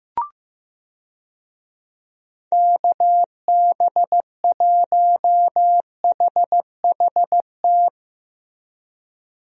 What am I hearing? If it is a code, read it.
Morse code, heard as a keyed tone, about 15 wpm, 700 Hz. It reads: KB1HHT
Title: USA Call Signs - 6 Characters - Repeated 3x - 15wpm